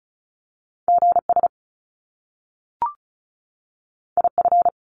Morse code, heard as a keyed tone, about 36 wpm, 700 Hz. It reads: GS IF
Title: Top Two Letter Combinations in Words 35wpm